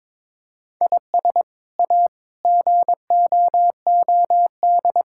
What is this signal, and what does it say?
Morse code, heard as a keyed tone, about 22 wpm, 700 Hz. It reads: IS A GOOD
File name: practice / 22wpm / Sets of 6 Words - 22wpm